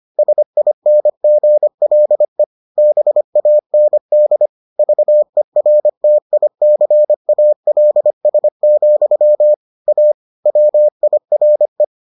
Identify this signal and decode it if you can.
Morse code, heard as a keyed tone, about 25 wpm, 600 Hz. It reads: SINGLE BAND VERTICALS, A WIRE